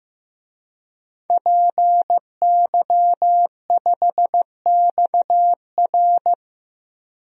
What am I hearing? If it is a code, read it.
Morse code, heard as a keyed tone, about 15 wpm, 700 Hz. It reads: PY5XR
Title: DXCC-100 Call Signs - 5 Characters - 15wpm